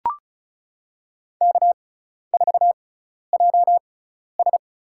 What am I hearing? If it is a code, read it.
Morse code, heard as a keyed tone, about 34 wpm, spaced as for 12 wpm, 700 Hz. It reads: K4JS
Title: USA Call Signs - 4 Characters - 6x Character Spacing - 35wpm